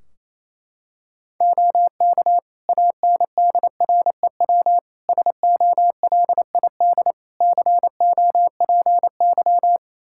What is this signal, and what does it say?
Morse code, heard as a keyed tone, about 28 wpm, 700 Hz. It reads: OK ANDREW SOLID COPY
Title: Practice QSO - Exchange 3 - More Info - 28wpm